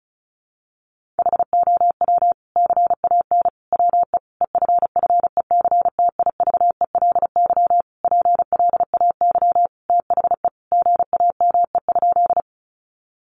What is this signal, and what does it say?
Morse code, heard as a keyed tone, about 34 wpm, 700 Hz. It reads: HOW CAN WE EFFECTIVELY PLAY THE GAME?
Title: Sentences from Top 1400 Words 35wpm